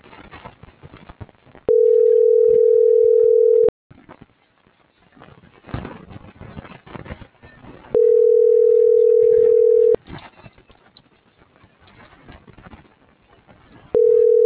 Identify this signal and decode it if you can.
Ringback tone (North America), 440 + 480 Hz.